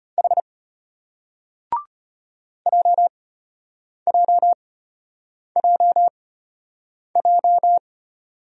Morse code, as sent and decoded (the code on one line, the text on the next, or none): .... / .--- .--- .--- .---
H JJJJ